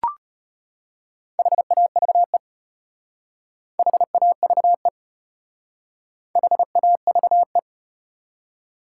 Morse code, heard as a keyed tone, about 33 wpm, 700 Hz. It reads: HAVE HAVE HAVE